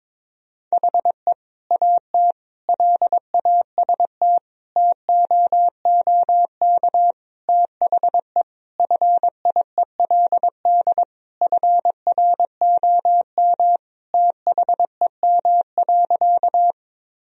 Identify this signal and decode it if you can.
Morse code, heard as a keyed tone, about 22 wpm, 700 Hz. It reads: HE AT LAST TOOK THE FIELD FROM THEM.
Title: Sentences from Top 1000 Words - Review Part 2 - 22wpm